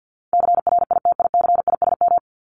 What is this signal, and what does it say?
Morse code, heard as a keyed tone, about 50 wpm, 700 Hz. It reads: CRITICISM
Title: Top 1001-1250 Words 50wpm